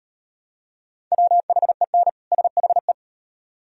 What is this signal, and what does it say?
Morse code, heard as a keyed tone, about 38 wpm, 700 Hz. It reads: WHEN SHE